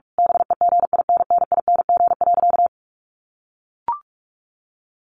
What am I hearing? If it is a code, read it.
Morse code, heard as a keyed tone, about 44 wpm, 700 Hz. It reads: BEGINNING.
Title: A Sound Of Thunder 45wpm